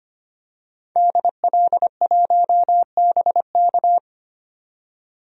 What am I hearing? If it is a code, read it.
Morse code, heard as a keyed tone, about 25 wpm, 700 Hz. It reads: DL1BK
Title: DXCC-100 Call Signs - 5 Characters - 25wpm